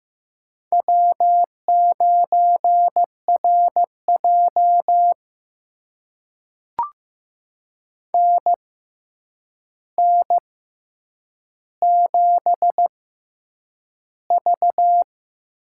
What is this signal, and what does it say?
Morse code, heard as a keyed tone, about 15 wpm, spaced as for 8 wpm, 700 Hz. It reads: W9RJ NN7V
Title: USA Call Signs - 4 Characters - 6x Character Spacing - 15wpm